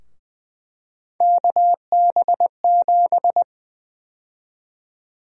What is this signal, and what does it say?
Morse code, heard as a keyed tone, about 21 wpm, 700 Hz. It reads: KB7
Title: USA Call Sign Prefixes - 20wpm